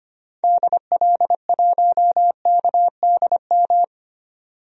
Morse code, heard as a keyed tone, about 25 wpm, 700 Hz. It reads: DL1KDM
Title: DXCC-100 Call Signs - 6 Characters - 25wpm